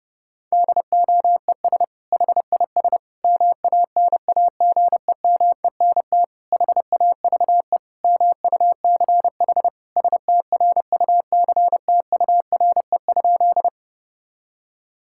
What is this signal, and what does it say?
Morse code, heard as a keyed tone, about 30 wpm, 700 Hz. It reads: DOES HIS MANAGEMENT HAVE MUCH STRUCTURE?
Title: Sentences from Top 1000 Words - Review Part 6 - 30wpm